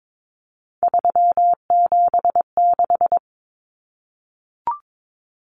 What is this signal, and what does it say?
Morse code, heard as a keyed tone, about 22 wpm, 700 Hz. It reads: 376
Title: Common Cut Numbers - 3-Digit 22wpm